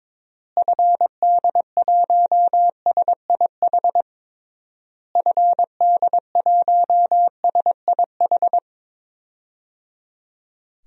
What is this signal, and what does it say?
Morse code, heard as a keyed tone, about 22 wpm, 700 Hz. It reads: FD1SIH FD1SIH
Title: DXCC-100 Call Signs - 6 Characters - Repeated 3x - 22wpm